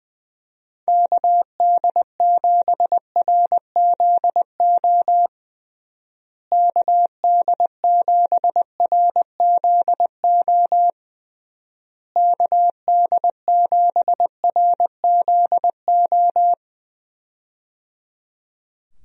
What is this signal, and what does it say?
Morse code, heard as a keyed tone, about 20 wpm, 700 Hz. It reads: KD7RZO KD7RZO KD7RZO
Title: USA Call Signs - 6 Characters - Repeated 3x - 20wpm